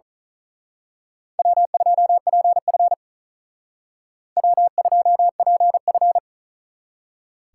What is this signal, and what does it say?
Morse code, heard as a keyed tone, about 39 wpm, 700 Hz. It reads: W2PF W2PF
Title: Speed Racing - USA Call Signs 35wpm